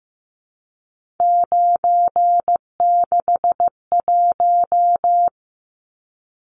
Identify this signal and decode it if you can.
Morse code, heard as a keyed tone, about 15 wpm, 700 Hz. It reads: 961
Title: Numbers - 3-Digit 15wpm